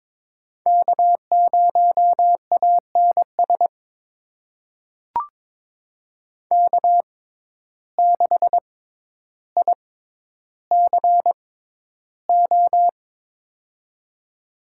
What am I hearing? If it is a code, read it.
Morse code, heard as a keyed tone, about 22 wpm, spaced as for 11 wpm, 700 Hz. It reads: K0ANS K6ICO
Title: USA Call Signs - 5 Characters - 6x Character Spacing - 22wpm